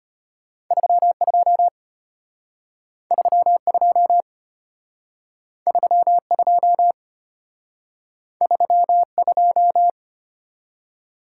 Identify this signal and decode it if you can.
Morse code, heard as a keyed tone, about 34 wpm, 700 Hz. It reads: 32 32 32 32